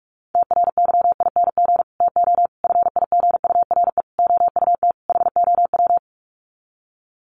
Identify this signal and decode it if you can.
Morse code, heard as a keyed tone, about 44 wpm, 700 Hz. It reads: TRYING TO FIGURE OUT HOW
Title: Sets of 5 Words Encore 45wpm